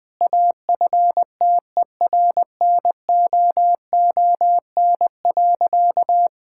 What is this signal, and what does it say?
Morse code, heard as a keyed tone, about 20 wpm, 700 Hz. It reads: AFTERNOON.
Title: Sentences from Top 1600 Words 20wpm